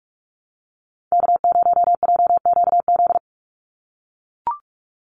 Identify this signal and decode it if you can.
Morse code, heard as a keyed tone, about 44 wpm, 700 Hz. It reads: K0JQZ
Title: NA SOTA Activator Callsigns 45wpm